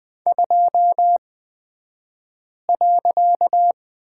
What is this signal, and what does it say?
Morse code, heard as a keyed tone, about 20 wpm, spaced as for 5 wpm, 700 Hz. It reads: 2.
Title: 254 Lesson 41d - Period - Two Characters 20wpm